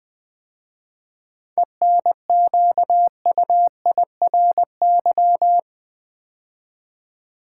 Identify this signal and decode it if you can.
Morse code, heard as a keyed tone, about 20 wpm, 700 Hz. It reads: ENQUIRY
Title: 7 Letter Words - First 3 Letters Given - 20wpm